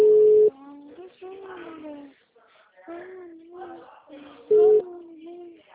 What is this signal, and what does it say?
Ringback tone (Europe, 425 Hz).
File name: out- -2003- - - .58